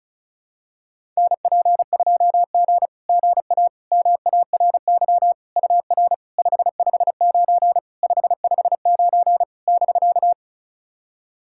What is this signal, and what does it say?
Morse code, heard as a keyed tone, about 36 wpm, 690 Hz. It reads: NP2G GA MARY UR 559 559 <BK>
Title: SOTA QSO Practice 35wpm